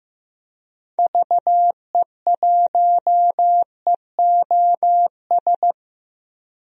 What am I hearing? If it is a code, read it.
Morse code, heard as a keyed tone, about 15 wpm, 700 Hz. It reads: VE1EOS
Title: DXCC-100 Call Signs - 6 Characters - Repeated 3x - 15wpm